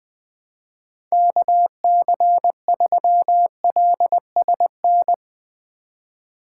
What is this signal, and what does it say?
Morse code, heard as a keyed tone, about 21 wpm, 700 Hz. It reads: KC3LSN